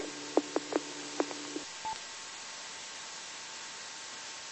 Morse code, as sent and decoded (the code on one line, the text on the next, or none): none